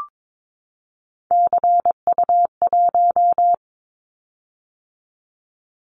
Morse code, heard as a keyed tone, about 22 wpm, 700 Hz. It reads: CU1
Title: Callsign Prefixes for Top 50 DXCC - 22wpm